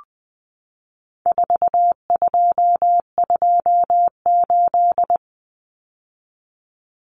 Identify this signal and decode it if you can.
Morse code, heard as a keyed tone, about 20 wpm, 700 Hz. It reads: 4228
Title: Numbers - 4-Digit 20wpm